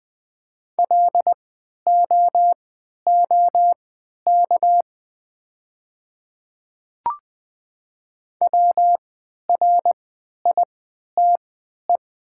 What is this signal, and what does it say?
Morse code, heard as a keyed tone, about 20 wpm, spaced as for 11 wpm, 690 Hz. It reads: LOOK WRITE
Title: 3x Character Spacing - Top 200 Words - 20wpm